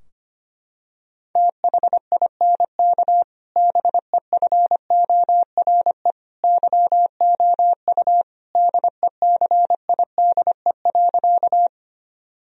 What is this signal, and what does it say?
Morse code, heard as a keyed tone, about 25 wpm, 700 Hz. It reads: THINK BEFORE YOU DECIDE.
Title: Sentences from Top 1000 Words - Review Part 3 - 25wpm